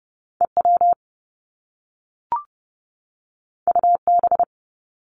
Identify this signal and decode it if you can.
Morse code, heard as a keyed tone, about 30 wpm, 700 Hz. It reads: EW UB
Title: Top Two Letter Combinations in Words 30wpm